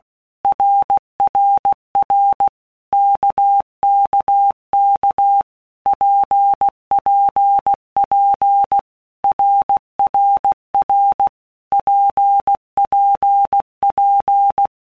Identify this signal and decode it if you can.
Morse code, heard as a keyed tone, about 16 wpm, 800 Hz. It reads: RRR KKK PPP RRR PPP
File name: Basic Letter Groups By Blocks / Groups@15 / RKPX@15